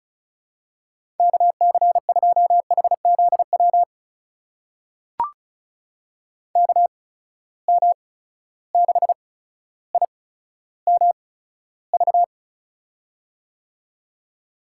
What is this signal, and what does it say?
Morse code, heard as a keyed tone, about 36 wpm, spaced as for 15 wpm, 700 Hz. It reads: KC2HZW KM6IMV
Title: USA Call Signs - 6 Characters - 8x Character Spacing - 35wpm